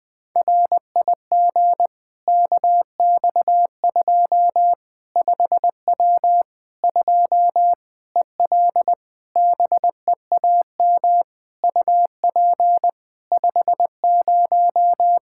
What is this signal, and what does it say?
Morse code, heard as a keyed tone, about 20 wpm, 700 Hz. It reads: RIG KX2 5W 2 EL BEAM UP 50
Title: Practice QSO - Exchange 2 - Info - 20wpm